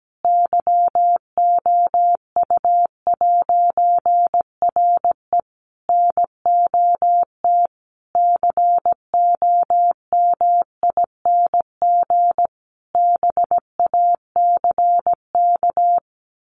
Morse code, heard as a keyed tone, about 17 wpm, 700 Hz. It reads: YOU'RE NOT COMING BACK
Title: A Sound Of Thunder 17wpm